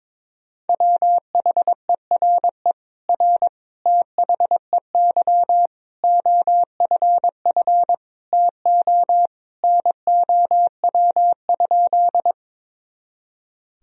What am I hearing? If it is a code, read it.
Morse code, heard as a keyed tone, about 22 wpm, 700 Hz. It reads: WHERE R THEY OFF TO NOW?
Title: Sentences from Top 200 Words and Abbreviations 22wpm